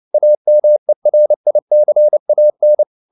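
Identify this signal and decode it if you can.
Morse code, heard as a keyed tone, about 29 wpm, 600 Hz. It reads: AMERICAN
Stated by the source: Morse code at 29 wpm, sending Ken Follett's novel Never in English, from CW med Gnister.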